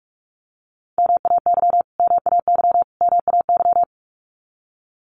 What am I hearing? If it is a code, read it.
Morse code, heard as a keyed tone, about 44 wpm, 700 Hz. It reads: MAY MAY MAY
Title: Top 500 Words Repeated 45wpm